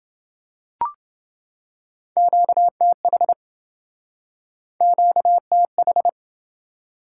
Morse code, heard as a keyed tone, about 27 wpm, 700 Hz. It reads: QTH QTH